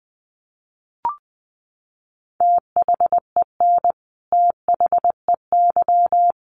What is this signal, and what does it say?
Morse code, heard as a keyed tone, about 20 wpm, 700 Hz. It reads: THEN THEY
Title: Sentences from Top 200 Words 20wpm